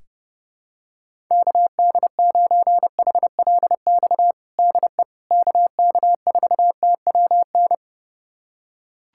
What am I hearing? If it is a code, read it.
Morse code, heard as a keyed tone, about 30 wpm, 700 Hz. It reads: KD9HLX DE KK4TWN
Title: USA Callsign DE Callsign - 6 Characters - 30wpm